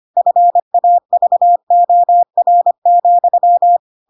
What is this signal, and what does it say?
Morse code, heard as a keyed tone, about 25 wpm, 700 Hz. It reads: FAVOR,
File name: Pinocchio / Pinocchio ch4 25 25 ik1hge